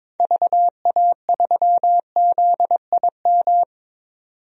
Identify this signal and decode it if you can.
Morse code, heard as a keyed tone, about 22 wpm, 700 Hz. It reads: VA3ZIM
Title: DXCC-100 Call Signs - 6 Characters - 22wpm